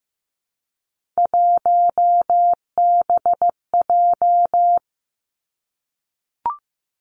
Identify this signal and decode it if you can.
Morse code, heard as a keyed tone, about 15 wpm, 700 Hz. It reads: 1BJ
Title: Three Challenging Letter-Number Combinations 15wpm